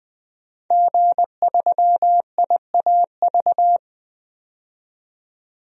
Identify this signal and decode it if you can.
Morse code, heard as a keyed tone, about 20 wpm, 700 Hz. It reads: G3IAV
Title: DXCC-100 Call Signs - 5 Characters - 20wpm